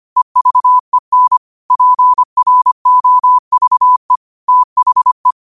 Morse code, heard as a keyed tone, about 25 wpm, 1000 Hz. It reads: EVEN PROVE THE